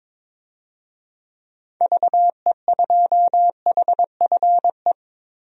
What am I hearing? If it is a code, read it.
Morse code, heard as a keyed tone, about 22 wpm, 700 Hz. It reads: VE2HFE
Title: DXCC-100 Call Signs - 6 Characters - Repeated 2x - 22wpm